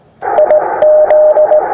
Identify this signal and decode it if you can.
Morse code, heard as a keyed tone, about 18 wpm, 610 Hz, with radio noise. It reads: IZ
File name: attachments / iz0aym 20m